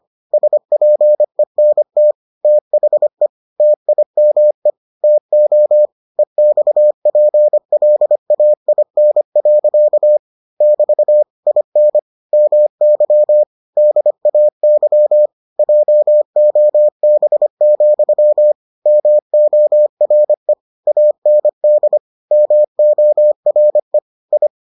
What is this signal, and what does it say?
Morse code, heard as a keyed tone, about 25 wpm, 600 Hz. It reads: SPENT THE TIME TO EXPLAIN. <BT> IN MY DAY JOB, MORE AND MORE I